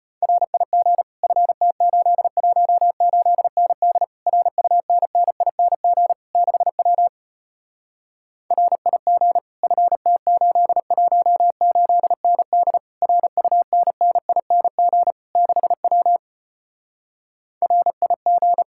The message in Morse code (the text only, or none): RIG FT818ND RUNNING 6W RIG FT818ND RUNNING 6W RIG